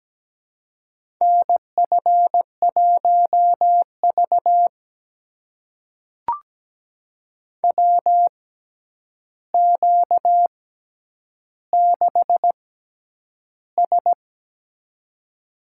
Morse code, heard as a keyed tone, about 17 wpm, spaced as for 9 wpm, 700 Hz. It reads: NF1V WQ6S
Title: USA Call Signs - 4 Characters - 6x Character Spacing - 17wpm